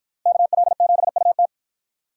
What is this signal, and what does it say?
Morse code, heard as a keyed tone, about 50 wpm, 700 Hz. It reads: KL7UT